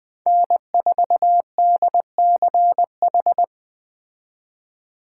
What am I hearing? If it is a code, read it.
Morse code, heard as a keyed tone, about 20 wpm, 700 Hz. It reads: N4DCH
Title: Speed Racing - USA 1x3 Call Signs 17wpm